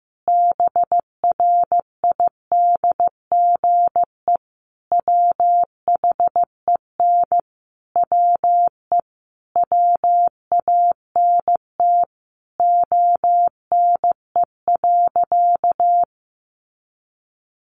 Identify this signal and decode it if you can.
Morse code, heard as a keyed tone, about 15 wpm, 700 Hz. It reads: BRIDGE WHEN WE WANT ONE.
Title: Sentences from Top 1200 Words 15wpm